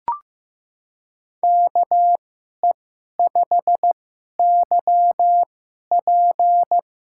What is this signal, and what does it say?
Morse code, heard as a keyed tone, about 15 wpm, 700 Hz. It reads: K E 5 Y P
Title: USA Call Signs - 6 Characters - 2x Character Spacing - 15wpm